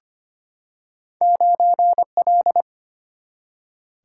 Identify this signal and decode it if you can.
Morse code, heard as a keyed tone, about 25 wpm, 700 Hz. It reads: 9L
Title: Single-Letter Single-Number 25wpm